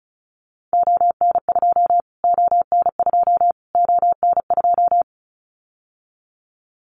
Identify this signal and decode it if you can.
Morse code, heard as a keyed tone, about 34 wpm, 700 Hz. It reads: ON2 ON2 ON2
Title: Callsign Prefixes for Top 25 DXCC - Repeated 3x - 35wpm